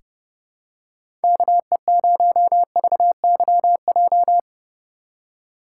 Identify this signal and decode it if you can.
Morse code, heard as a keyed tone, about 31 wpm, 700 Hz. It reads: KE0VYJ
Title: USA Call Signs - 6 Characters - 2x Character Spacing - 30wpm